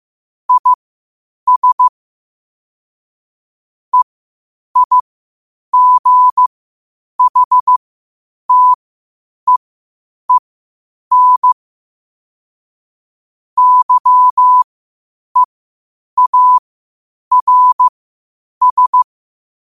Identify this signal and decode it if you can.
Morse code, heard as a keyed tone, about 15 wpm, spaced as for 8 wpm, 1000 Hz. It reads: IS EIGHTEEN YEARS
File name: qotdfiles / 8wpm